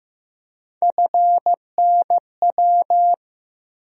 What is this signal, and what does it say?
Morse code, heard as a keyed tone, about 15 wpm, 700 Hz. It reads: FNW
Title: USA Call Sign Postfixes - 3 Letters 15wpm